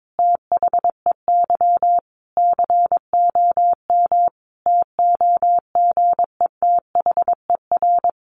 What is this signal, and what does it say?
Morse code, heard as a keyed tone, about 22 wpm, 700 Hz. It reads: THEY COM TOGETHER
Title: Sentences from Top 1300 Words 22wpm